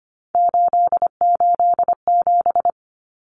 Morse code, heard as a keyed tone, about 25 wpm, 700 Hz. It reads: 887
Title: Numbers - 3-Digit 25wpm